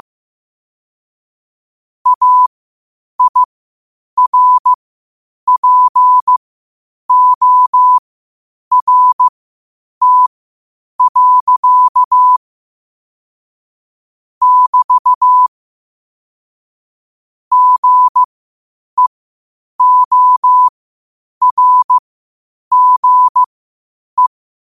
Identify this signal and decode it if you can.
Morse code, heard as a keyed tone, about 15 wpm, spaced as for 8 wpm, 1000 Hz. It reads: AIRPORT. <BT> GEORGE
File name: qotdfiles / 8wpm